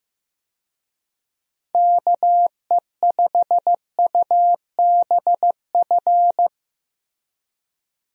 Morse code, heard as a keyed tone, about 15 wpm, 700 Hz. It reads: KE5UBF
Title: USA Call Signs - 6 Characters - Repeated 2x - 15wpm